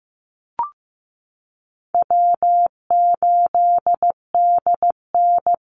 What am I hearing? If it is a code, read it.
Morse code, heard as a keyed tone, about 15 wpm, 700 Hz. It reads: W8DN
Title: Easy Call Signs Repeated 2x - 15wpm